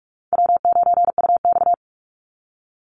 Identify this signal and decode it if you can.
Morse code, heard as a keyed tone, about 45 wpm, 700 Hz. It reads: W9UX
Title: NA SOTA Chaser Callsigns 45wpm